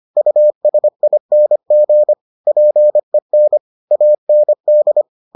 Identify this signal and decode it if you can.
Morse code, heard as a keyed tone, about 25 wpm, 600 Hz. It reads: USING PEN AND